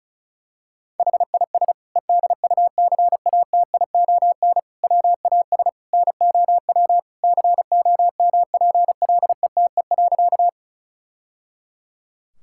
Morse code, heard as a keyed tone, about 36 wpm, 700 Hz. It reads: HIS EDUCATION WAS NOW COMPLETE.
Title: Sentences from Top 1000 Words - Review Part 6 - 35wpm